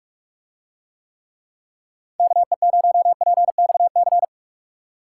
Morse code, heard as a keyed tone, about 47 wpm, 700 Hz. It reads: KE0PXC